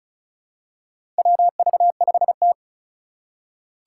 Morse code, heard as a keyed tone, about 36 wpm, 700 Hz. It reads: WV5T
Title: Four Challenging Letter-Number Combinations 35wpm